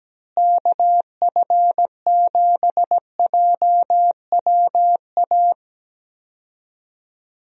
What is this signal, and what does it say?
Morse code, heard as a keyed tone, about 17 wpm, 700 Hz. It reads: KF7JWA
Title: USA Call Signs - 6 Characters - Repeated 2x - 17wpm